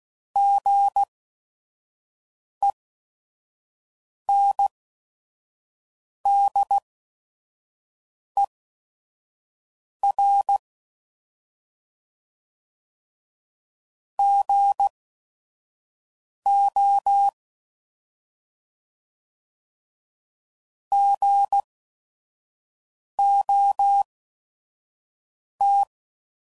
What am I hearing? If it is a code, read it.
Morse code, heard as a keyed tone, about 16 wpm, spaced as for 5 wpm, 800 Hz. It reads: GENDER GO GOT
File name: Prefixes / By First Letter / Gs